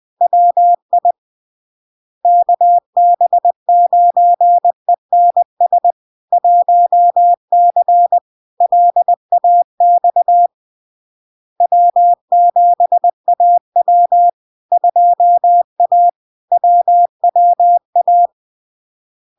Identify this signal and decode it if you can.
Morse code, heard as a keyed tone, about 20 wpm, 700 Hz. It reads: WI KB9ENS 1C LAX W7AW 2A WWA
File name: LARK-FD-2022-Log-in-CW-20wpm-700-Hz